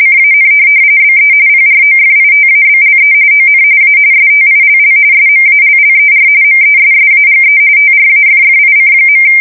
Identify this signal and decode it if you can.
RTTY, about 45.45 baud. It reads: CAN BE PROFITABLE ON A STILL SMALLER MESSAGE TRAFFIC.
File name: fsk170 / printing-telegraph